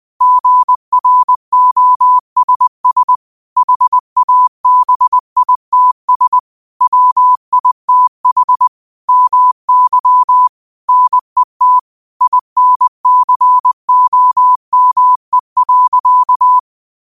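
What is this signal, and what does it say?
Morse code, heard as a keyed tone, about 20 wpm, 1000 Hz. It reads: GROSS HABITS WITH MY NET INCOME.
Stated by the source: Morse code quote of the day at 20 wpm.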